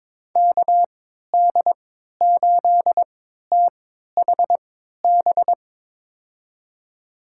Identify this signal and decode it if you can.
Morse code, heard as a keyed tone, about 22 wpm, spaced as for 12 wpm, 700 Hz. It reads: KD8THB
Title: USA Call Signs - 6 Characters - 3x Character Spacing - 22wpm